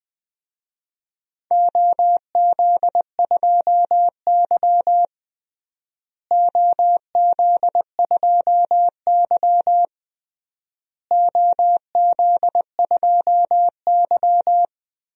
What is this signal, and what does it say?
Morse code, heard as a keyed tone, about 20 wpm, 700 Hz. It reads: OZ2Y OZ2Y OZ2Y
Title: DXCC-100 Call Signs - 4 Characters - Repeated 3x - 20wpm